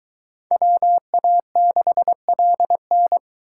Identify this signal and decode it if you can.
Morse code, heard as a keyed tone, about 22 wpm, 700 Hz. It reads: WA6LN